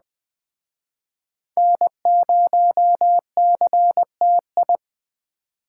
Morse code, heard as a keyed tone, about 21 wpm, 700 Hz. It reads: N0CTI